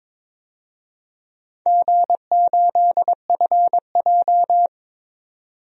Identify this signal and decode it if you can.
Morse code, heard as a keyed tone, about 22 wpm, 700 Hz. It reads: G8FJ
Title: DXCC-100 Call Signs - 4 Characters - 22wpm